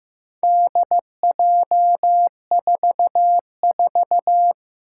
Morse code, heard as a keyed tone, about 15 wpm, 700 Hz. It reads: DJ44
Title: Four Challenging Letter-Number Combinations 15wpm